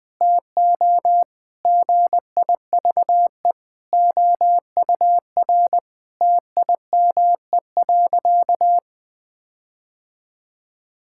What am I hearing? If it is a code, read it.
Morse code, heard as a keyed tone, about 20 wpm, 700 Hz. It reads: TO GIVE OUR TIME.
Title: Sentences with 4-Word Phrases - 20wpm